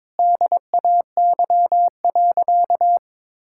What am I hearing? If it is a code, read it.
Morse code, heard as a keyed tone, about 22 wpm, 700 Hz. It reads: DAY.
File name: Difficult Sentences with Hints - 22wpm